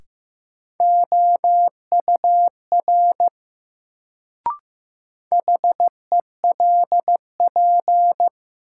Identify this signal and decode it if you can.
Morse code, heard as a keyed tone, about 15 wpm, 700 Hz. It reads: OUR HELP
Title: Sentences from Top 200 Words - New Word Drill - 15wpm